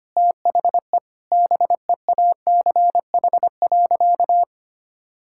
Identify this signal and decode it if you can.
Morse code, heard as a keyed tone, about 25 wpm, 700 Hz. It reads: THE BEACH.
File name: practice / 25wpm / Difficult Sentences with Hints - 25wpm